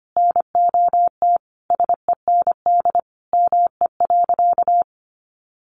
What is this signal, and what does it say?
Morse code, heard as a keyed tone, about 25 wpm, 700 Hz. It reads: NOT SEND ME.
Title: Sentences from Top 1500 Words 25wpm